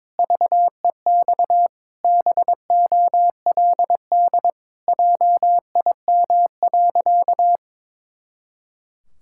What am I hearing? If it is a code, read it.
Morse code, heard as a keyed tone, about 22 wpm, 700 Hz. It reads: VEX BOLD JIM.
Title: Pangram Challenge v2 - 22wpm